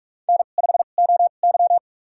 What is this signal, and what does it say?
Morse code, heard as a keyed tone, about 46 wpm, spaced as for 33 wpm, 700 Hz. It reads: N5OY
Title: USA Call Signs - 4 Characters - 2x Character Spacing - 45wpm